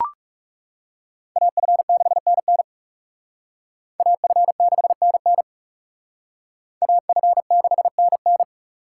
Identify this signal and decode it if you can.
Morse code, heard as a keyed tone, about 36 wpm, 700 Hz. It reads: AF6NN AF6NN AF6NN
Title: Speed Racing - USA Call Signs 30wpm